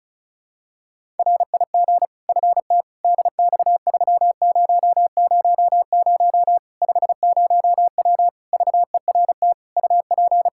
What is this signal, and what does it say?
Morse code, heard as a keyed tone, about 36 wpm, 700 Hz. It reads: RIG FT DX3000 50W VERT UP
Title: Practice QSO - Exchange 2 - Info - 35wpm